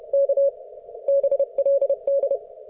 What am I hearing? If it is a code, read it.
Morse code, heard as a keyed tone, about 31 wpm, 560 Hz. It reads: K BLD